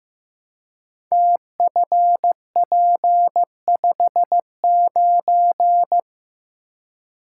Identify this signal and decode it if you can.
Morse code, heard as a keyed tone, about 15 wpm, 700 Hz. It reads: TFP59
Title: Code Groups - 5 Letter:Numbers - 15wpm